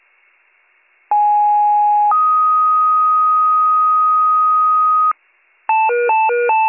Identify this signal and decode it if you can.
Two-tone page, heard as a one-second tone, then a three-second tone, about 830 then 1250 Hz. It is heard from 1.1 to 5.1 s.